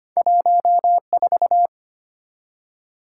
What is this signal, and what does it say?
Morse code, heard as a keyed tone, about 25 wpm, 700 Hz. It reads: 14
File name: Two Letter-Number Combinations - 25wpm